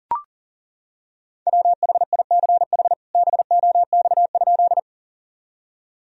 Morse code, heard as a keyed tone, about 40 wpm, 700 Hz. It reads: WHICH BOX?
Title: Sentences from Top 1000 Words - Review Part 2 - 40wpm